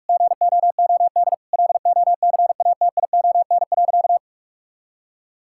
Morse code, heard as a keyed tone, about 46 wpm, 700 Hz. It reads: GOOD LOCATION.
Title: Sentences from Top 1000 Words - Review Part 6 - 45wpm